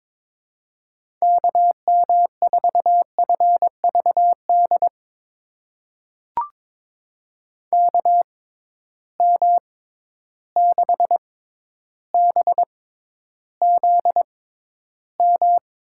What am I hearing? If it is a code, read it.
Morse code, heard as a keyed tone, about 22 wpm, spaced as for 11 wpm, 700 Hz. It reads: KM4FVD KM6BZM
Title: USA Call Signs - 6 Characters - 6x Character Spacing - 22wpm